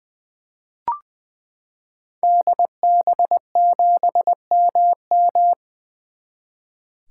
Morse code, heard as a keyed tone, about 21 wpm, 700 Hz. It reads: DB7MM
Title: 2023 SOTA Activator Call Signs - 20wpm